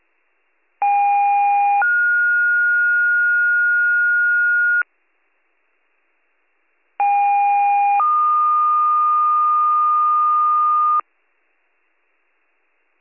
Two-tone page, heard as a one-second tone, then a three-second tone, about 810 then 1430 Hz, about 810 then 1220 Hz. It is heard from 0.8 to 11.0 s.